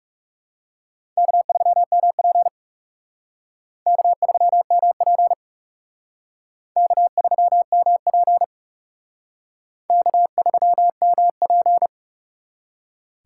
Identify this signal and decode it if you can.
Morse code, heard as a keyed tone, about 37 wpm, 700 Hz. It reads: K3MP K3MP K3MP K3MP
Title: Speed Racing - USA 1x2 Call Signs 30wpm